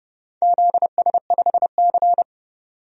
Morse code, heard as a keyed tone, about 30 wpm, 700 Hz. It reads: ZS5C
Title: DXCC-100 Call Signs - 4 Characters - 30wpm